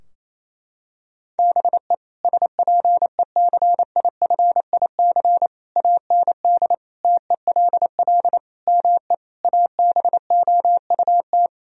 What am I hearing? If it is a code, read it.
Morse code, heard as a keyed tone, about 28 wpm, 700 Hz. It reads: BE SPECIFIC AND TELL ME ABOUT
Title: Sentences from Top 1000 Words - Review Part 3 - 28wpm